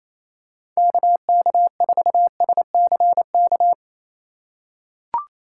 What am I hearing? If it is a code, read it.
Morse code, heard as a keyed tone, about 28 wpm, 700 Hz. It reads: KK4SCK
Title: USA Call Signs - 6 Characters - 3x Character Spacing - 28wpm